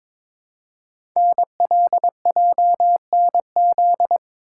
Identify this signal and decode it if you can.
Morse code, heard as a keyed tone, about 22 wpm, 700 Hz. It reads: NLJNZ
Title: Code Groups - 5 Letters - 22wpm